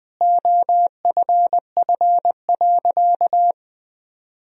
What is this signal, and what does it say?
Morse code, heard as a keyed tone, about 20 wpm, 700 Hz. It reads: OFF.
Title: Sentences from Top 1000 Words - Review Part 3 - 20wpm